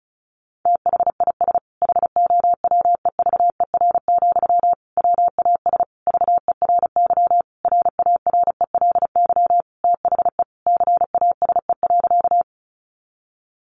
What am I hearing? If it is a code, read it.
Morse code, heard as a keyed tone, about 34 wpm, 700 Hz. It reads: THIS HOWEVER, WAS VERY RARELY THE CASE.
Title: Sentences from Top 1400 Words 35wpm